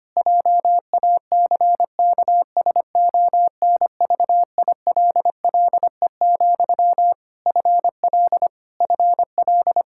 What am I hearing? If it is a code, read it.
Morse code, heard as a keyed tone, about 25 wpm, 700 Hz. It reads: JACKSONVILLE, FL FL
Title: Practice QSO - Exchange 1 - Beginning - 25wpm